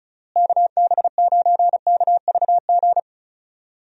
Morse code, heard as a keyed tone, about 34 wpm, 700 Hz. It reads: KB9KVG
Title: USA Call Signs - 6 Characters - Repeated 3x - 35wpm